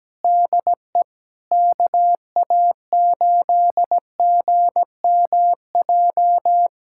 Morse code, heard as a keyed tone, about 17 wpm, 700 Hz. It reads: DE KA8GMJ
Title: USA Callsign DE Callsign - 6 Characters - 17wpm